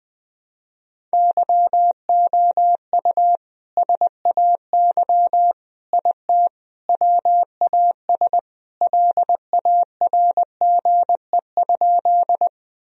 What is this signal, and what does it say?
Morse code, heard as a keyed tone, about 21 wpm, 700 Hz. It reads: YOU SAY IT WAS LARGE?
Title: Sentences from Top 1000 Words - Review Part 1 - 20wpm